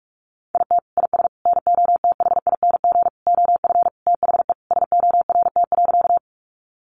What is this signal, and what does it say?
Morse code, heard as a keyed tone, about 44 wpm, 700 Hz. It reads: IT IS NOTHING OF THE SORT.
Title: Sentences from Top 1300 Words 45wpm